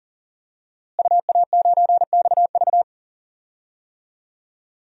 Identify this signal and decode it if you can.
Morse code, heard as a keyed tone, about 40 wpm, 700 Hz. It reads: UA9XV
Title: DXCC-100 Call Signs - 5 Characters - Repeated 3x - 40wpm